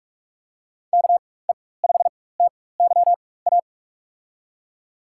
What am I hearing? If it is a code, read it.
Morse code, heard as a keyed tone, about 47 wpm, spaced as for 21 wpm, 700 Hz. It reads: KE5TYA
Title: USA Call Signs - 6 Characters - 4x Character Spacing - 45wpm